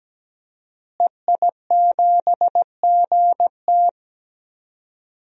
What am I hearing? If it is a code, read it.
Morse code, heard as a keyed tone, about 17 wpm, 700 Hz. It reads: EI7GT
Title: DXCC-100 Call Signs - 5 Characters - Repeated 3x - 17wpm